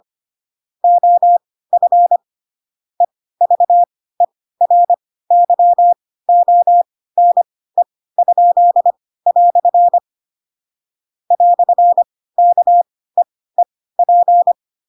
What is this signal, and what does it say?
Morse code, heard as a keyed tone, about 25 wpm, spaced as for 16 wpm, 700 Hz. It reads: OF EVERYONE?" "KEEP
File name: Pinocchio / Pinocchio ch4 10 25 ik1hge